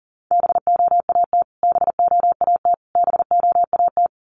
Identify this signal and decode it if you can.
Morse code, heard as a keyed tone, about 40 wpm, 700 Hz. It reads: BOAT BOAT BOAT
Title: Top 500 Words Repeated 40wpm